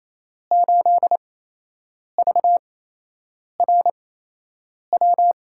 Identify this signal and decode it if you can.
Morse code, heard as a keyed tone, about 28 wpm, spaced as for 8 wpm, 700 Hz. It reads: 8VRW
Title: USA Call Signs - 6 Characters - 8x Character Spacing - 28wpm